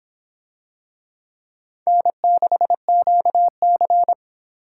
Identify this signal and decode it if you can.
Morse code, heard as a keyed tone, about 24 wpm, 700 Hz. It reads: N6QC